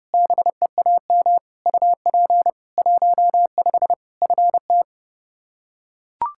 Morse code, heard as a keyed tone, about 30 wpm, 700 Hz. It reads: BEAM UP 15 FT
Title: K1USN SST QSO Practice 30wpm